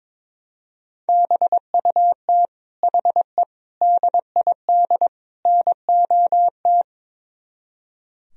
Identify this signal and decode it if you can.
Morse code, heard as a keyed tone, about 22 wpm, 700 Hz. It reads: BUT HE DID NOT
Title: Speed-Racing - Sets of 4 Words 22wpm